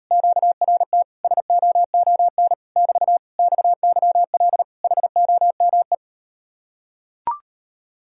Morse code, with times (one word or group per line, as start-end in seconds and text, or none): 0.1-1.0 QRT
1.2-2.5 SOON
2.8-3.2 <BT>
3.4-4.6 XYL
4.8-5.9 HOME